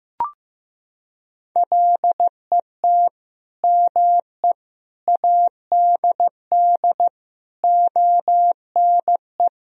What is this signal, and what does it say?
Morse code, heard as a keyed tone, about 15 wpm, 700 Hz. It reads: LET ME ADD ONE
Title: Sentences from Top 1000 Words - Review Part 1 - 15wpm